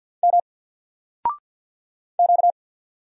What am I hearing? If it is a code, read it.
Morse code, heard as a keyed tone, about 50 wpm, 700 Hz. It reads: M Q
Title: Single Letter-Number Alt 50wpm